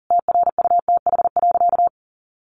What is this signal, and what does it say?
Morse code, heard as a keyed tone, about 40 wpm, 700 Hz. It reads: TRUTH.
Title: Sentences from Top 800 Words 40wpm